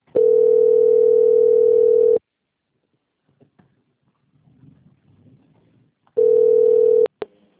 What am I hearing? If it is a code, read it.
Ringback tone (North America), 440 + 480 Hz.